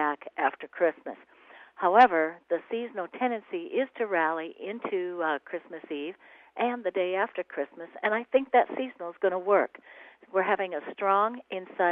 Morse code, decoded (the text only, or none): none